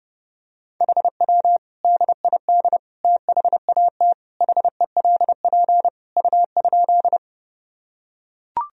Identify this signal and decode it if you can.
Morse code, heard as a keyed tone, about 32 wpm, 700 Hz. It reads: HW DID THAT HELP U?
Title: Sentences from Top 200 Words and Abbreviations 30wpm